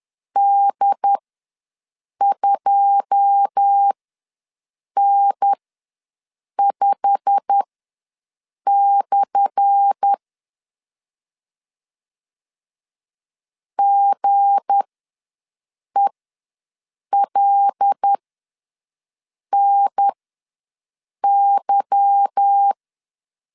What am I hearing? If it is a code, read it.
Morse code, heard as a keyed tone, about 10 wpm, spaced as for 6 wpm, 800 Hz. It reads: D2N5/ GELNY
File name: LEZIONI / 10 SIGNS / 10A SLASH ws 10wpm 4121